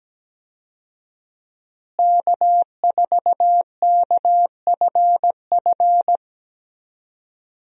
Morse code, heard as a keyed tone, about 17 wpm, 700 Hz. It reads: K4KFF
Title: USA Call Signs - 5 Characters - Repeated 3x - 17wpm